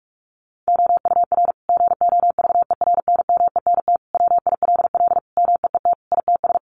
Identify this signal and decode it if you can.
Morse code, heard as a keyed tone, about 44 wpm, 700 Hz. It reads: OUR GOVERNMENT WILL MEET ITS
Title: Sentences from Top 1200 Words 45wpm